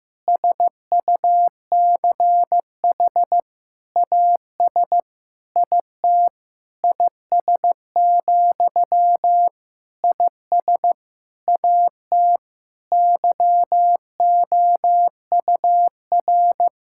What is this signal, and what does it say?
Morse code, heard as a keyed tone, about 15 wpm, 700 Hz. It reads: SUCH AS IT IS, IS AT YOUR
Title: Sentences from Top 1600 Words 15wpm